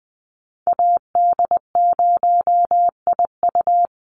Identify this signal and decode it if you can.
Morse code, heard as a keyed tone, about 20 wpm, 700 Hz. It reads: AD0IU
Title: Call Signs - 20wpm